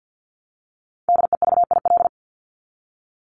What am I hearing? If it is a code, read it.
Morse code, heard as a keyed tone, about 50 wpm, 700 Hz. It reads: DEVIL
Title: Top 1751-2000 Words 50wpm